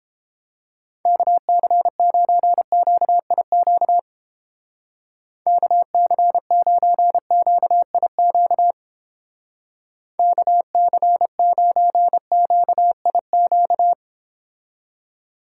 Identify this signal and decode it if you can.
Morse code, heard as a keyed tone, about 28 wpm, 700 Hz. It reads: KC9QIQ KC9QIQ KC9QIQ